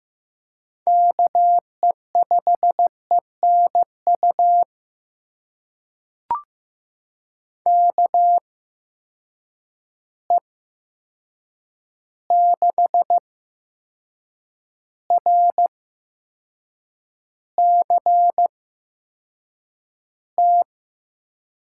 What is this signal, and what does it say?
Morse code, heard as a keyed tone, about 15 wpm, spaced as for 6 wpm, 700 Hz. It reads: KE5ENU KE6RCT
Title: USA Call Signs - 6 Characters - 8x Character Spacing - 15wpm